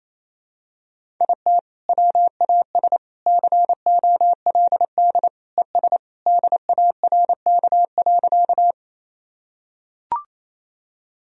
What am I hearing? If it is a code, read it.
Morse code, heard as a keyed tone, about 28 wpm, 700 Hz. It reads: IT WAS COLD ES DARK.
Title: Sentences from Top 500 Words and Abbreviations 28wpm